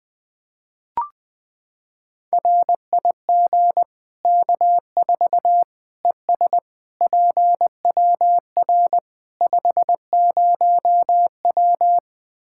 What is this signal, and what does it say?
Morse code, heard as a keyed tone, about 20 wpm, 700 Hz. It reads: RIG K4 ES PWR 50W
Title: K1USN SST QSO Practice 20wpm